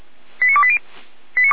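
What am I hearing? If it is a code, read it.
Five-tone selective call (ZVEI), heard as quick runs of five tones, tones of about 70 ms. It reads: 89280
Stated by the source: ZVEI five-tone selective call received via SDR.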